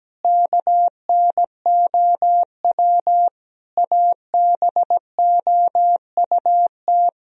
Morse code, heard as a keyed tone, about 17 wpm, 700 Hz. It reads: KNOW ABOUT
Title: Sentences from Top 1000 Words - Review Part 1 - 17wpm